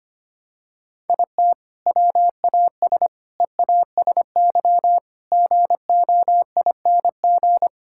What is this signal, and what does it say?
Morse code, heard as a keyed tone, about 25 wpm, 700 Hz. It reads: IT WAS EASY GOING
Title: Sentences from Top 1000 Words - Review Part 5 - 25wpm